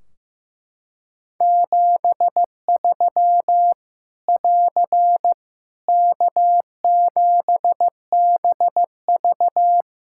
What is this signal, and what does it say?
Morse code, heard as a keyed tone, about 15 wpm, 700 Hz. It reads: 73 <AR> K7BV
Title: Practice QSO - Exchange 4 - Ending - 15wpm